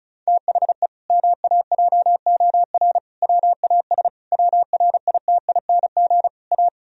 Morse code, heard as a keyed tone, about 36 wpm, 700 Hz. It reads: THE MAJOR WAS WRITING A
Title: Sentences from Top 1000 Words - Review Part 4 - 35wpm